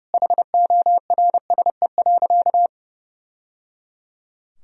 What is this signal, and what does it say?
Morse code, heard as a keyed tone, about 30 wpm, 700 Hz. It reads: HORSE.
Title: Sentences from Top 1000 Words - Review Part 4 - 30wpm